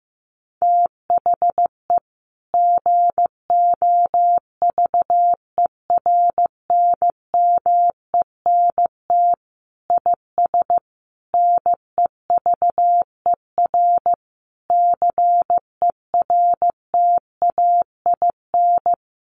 Morse code, heard as a keyed tone, about 15 wpm, 700 Hz. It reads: THE GOVERNMENT IS NEVER CERTAIN
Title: Sentences from Top 1300 Words 15wpm